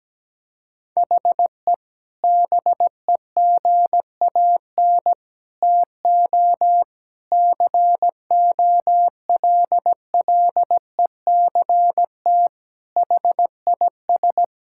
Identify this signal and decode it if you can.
Morse code, heard as a keyed tone, about 17 wpm, 700 Hz. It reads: HE BEGAN TO COLLECT HIS
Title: Sentences from Top 1600 Words 17wpm